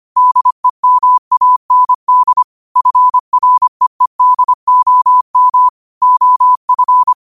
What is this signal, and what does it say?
Morse code, heard as a keyed tone, about 25 wpm, 1000 Hz. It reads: DEMAND FREEDOM OF
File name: qotdfiles / 25wpm